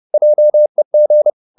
Morse code, heard as a keyed tone, about 30 wpm, 600 Hz. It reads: JEG
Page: Ferie 30 wpm